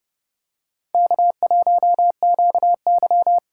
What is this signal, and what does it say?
Morse code, heard as a keyed tone, about 28 wpm, 700 Hz. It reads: K1QY